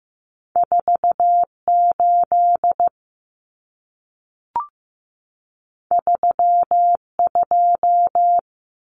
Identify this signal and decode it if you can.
Morse code, heard as a keyed tone, about 15 wpm, 700 Hz. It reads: 48 32
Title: Common Cut Numbers - 2-Digit 15wpm